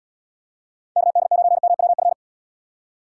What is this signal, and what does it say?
Morse code, heard as a keyed tone, about 78 wpm, 700 Hz. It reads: KD9MWF